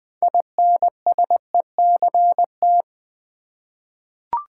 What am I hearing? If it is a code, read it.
Morse code, heard as a keyed tone, about 20 wpm, 700 Hz. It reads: INSECT
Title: Six Letter Words - 20wpm